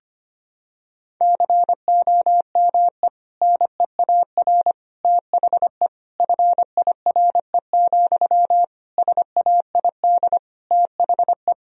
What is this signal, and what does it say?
Morse code, heard as a keyed tone, about 25 wpm, 700 Hz. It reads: COME NEAR THE FIRE, SAID THE
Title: Sentences from Top 1000 Words - Review Part 1 - 25wpm